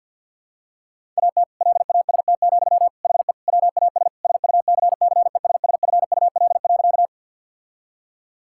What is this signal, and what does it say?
Morse code, heard as a keyed tone, about 50 wpm, 700 Hz. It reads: AT LAST, HE WAS SUCCESSFUL.
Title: Sentences from Top 1000 Words - Review Part 6 - 50wpm